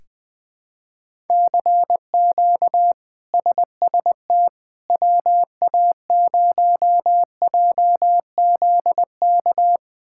Morse code, heard as a keyed tone, about 20 wpm, 700 Hz. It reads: CQ SST WA0JZK
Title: K1USN SST QSO Practice - 20wpm